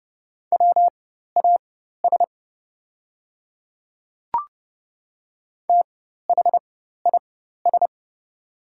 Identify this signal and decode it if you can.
Morse code, heard as a keyed tone, about 30 wpm, spaced as for 14 wpm, 690 Hz. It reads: WAS THIS
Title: 4x Character Spacing - Top 100 Words - 30wpm